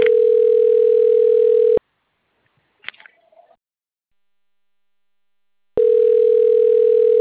Ringback tone (North America), 440 + 480 Hz.